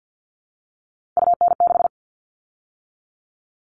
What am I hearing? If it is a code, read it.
Morse code, heard as a keyed tone, about 48 wpm, 700 Hz. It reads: UN6
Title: All Cut Numbers - 3-Digit 50wpm